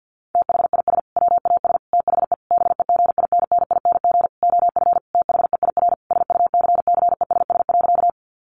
Morse code, heard as a keyed tone, about 49 wpm, 700 Hz. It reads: THIS WAS THE BEGINNING OF THEIR SUCCESS.
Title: Sentences from Top 900 Words 50wpm